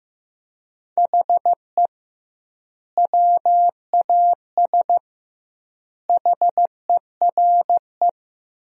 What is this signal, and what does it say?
Morse code, heard as a keyed tone, about 16 wpm, 700 Hz. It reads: HE WAS HERE
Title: 2x Word-Spacing - Sentences from Top 500 Words - 15wpm